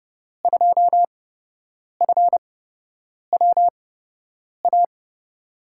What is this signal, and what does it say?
Morse code, heard as a keyed tone, about 31 wpm, spaced as for 8 wpm, 700 Hz. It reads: 2FWA
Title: USA Call Signs - 6 Characters - 8x Character Spacing - 30wpm